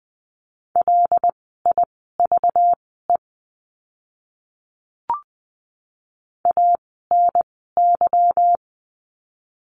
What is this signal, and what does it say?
Morse code, heard as a keyed tone, about 20 wpm, spaced as for 15 wpm, 700 Hz. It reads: LIVE ANY
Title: 192 Lesson 31f - Letter v - Words 20wpm